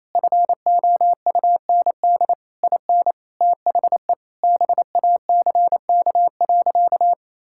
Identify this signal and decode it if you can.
Morse code, heard as a keyed tone, about 27 wpm, 700 Hz. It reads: FOUND IN THE BACK.
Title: Sentences from Top 1000 Words - Review Part 5 - 28wpm